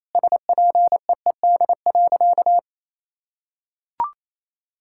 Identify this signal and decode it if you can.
Morse code, heard as a keyed tone, about 28 wpm, 700 Hz. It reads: SPEED.
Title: Sentences from Top 1000 Words - Review Part 5 - 28wpm